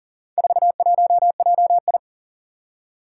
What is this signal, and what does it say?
Morse code, heard as a keyed tone, about 38 wpm, 700 Hz. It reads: 41JI
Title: Four Challenging Letter-Number Combinations 40wpm